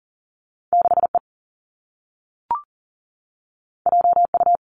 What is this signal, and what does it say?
Morse code, heard as a keyed tone, about 40 wpm, 700 Hz. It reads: 6E JU
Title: Two Challenging Letter-Number Combinations 40wpm